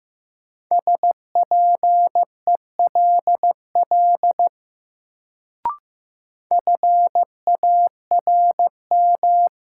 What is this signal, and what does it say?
Morse code, heard as a keyed tone, about 15 wpm, 700 Hz. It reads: SPELL FARM
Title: Sentences from Top 1200 Words - New Word Drill Alt - 15wpm